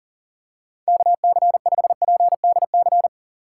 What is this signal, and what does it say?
Morse code, heard as a keyed tone, about 40 wpm, 700 Hz. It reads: KC5PDC
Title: USA Call Signs - 6x Character Spacing - 40wpm